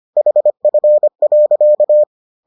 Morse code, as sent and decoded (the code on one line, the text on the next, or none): .... ..-. .-.-.-
HF.